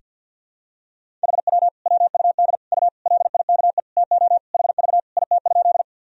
Morse code, heard as a keyed tone, about 50 wpm, 700 Hz. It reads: HW WUD U LIKE TO HV IT?
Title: Sentences from Top 100 Words and Abbreviations 50wpm